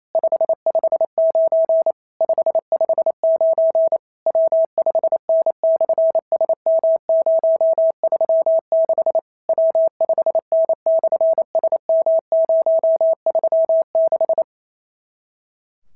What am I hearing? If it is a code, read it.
Morse code, heard as a keyed tone, about 27 wpm, 650 Hz. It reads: 559 559 W5N/SM036 W5N/SM036
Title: SOTA QSO Practice 28wpm